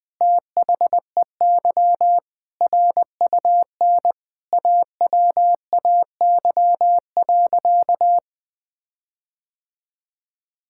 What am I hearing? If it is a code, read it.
Morse code, heard as a keyed tone, about 19 wpm, 700 Hz. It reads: THEY RUN AWAY.
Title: Sentences from Top 1600 Words 20wpm